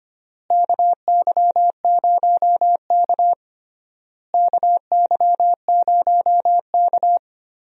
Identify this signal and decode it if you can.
Morse code, heard as a keyed tone, about 25 wpm, 700 Hz. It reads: KY0K KY0K
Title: USA Call Signs - 4 Characters - Repeated 3x - 25wpm